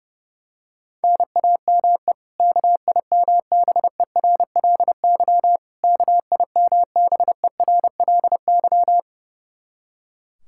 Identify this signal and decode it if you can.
Morse code, heard as a keyed tone, about 29 wpm, 700 Hz. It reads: NAME KIMBERLY KIMBERLY
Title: Practice QSO - Exchange 1 - Beginning - 30wpm